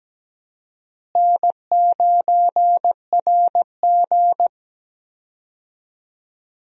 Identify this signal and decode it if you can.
Morse code, heard as a keyed tone, about 17 wpm, 700 Hz. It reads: N9RG
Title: USA Call Signs - 4 Characters - Repeated 3x - 17wpm